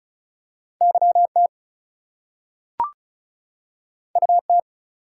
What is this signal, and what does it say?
Morse code, heard as a keyed tone, about 36 wpm, 700 Hz. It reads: YT UT
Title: US States and Canadian Provinces 35wpm